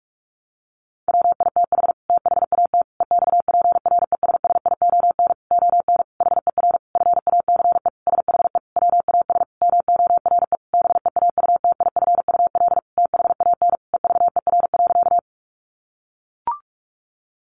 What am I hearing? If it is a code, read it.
Morse code, heard as a keyed tone, about 44 wpm, 700 Hz. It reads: WITH THAT EXPRESSION ON HER FACE SHE WAS MORE BEAUTIFUL THAN EVER.
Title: Sentences from Top 1200 Words 45wpm